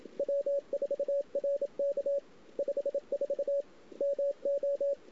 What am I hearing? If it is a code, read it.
Morse code, heard as a keyed tone, about 27 wpm, 560 Hz. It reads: W4RK 54 MO
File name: CW / 2008-ss-cw-k5tr-op-n5rz -01:00